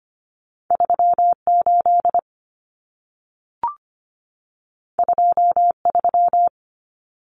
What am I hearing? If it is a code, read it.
Morse code, heard as a keyed tone, about 25 wpm, 700 Hz. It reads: 38 23
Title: Common Cut Numbers - 2-Digit 25wpm